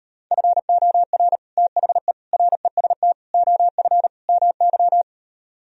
Morse code, heard as a keyed tone, about 38 wpm, 700 Hz. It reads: FOR THE REST OF MY